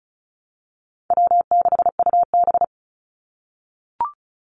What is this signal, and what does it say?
Morse code, heard as a keyed tone, about 34 wpm, 700 Hz. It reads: W6UB
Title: NA SOTA Chaser Callsigns 35wpm